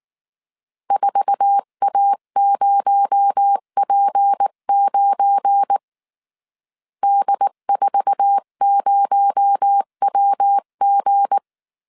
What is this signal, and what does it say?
Morse code, heard as a keyed tone, about 19 wpm, 800 Hz. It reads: 4A0P9 D40WG
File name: CW / DET / 1 9 1500C S B 18wpm 2551